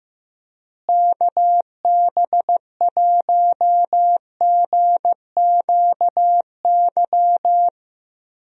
Morse code, heard as a keyed tone, about 16 wpm, 700 Hz. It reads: KB1GQY
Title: USA Call Signs - 6 Characters - Repeated 3x - 15wpm